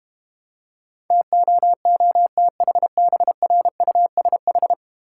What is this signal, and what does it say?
Morse code, heard as a keyed tone, about 32 wpm, 700 Hz. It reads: TOOTHBRUSH